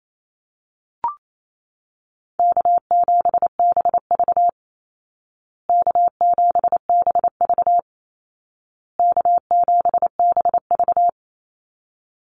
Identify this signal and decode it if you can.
Morse code, heard as a keyed tone, about 28 wpm, 700 Hz. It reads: K7BV K7BV K7BV
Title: Call Signs Repeated 2x - 28wpm